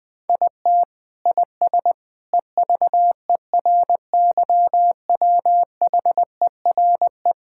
Morse code, heard as a keyed tone, about 20 wpm, 700 Hz. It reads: IT IS EVERYWHERE
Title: Sentences from Top 1600 Words 20wpm